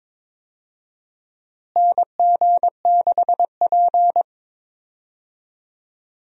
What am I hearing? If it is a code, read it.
Morse code, heard as a keyed tone, about 22 wpm, 700 Hz. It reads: NG6P
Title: USA Call Signs - 4 Characters - Repeated 3x - 22wpm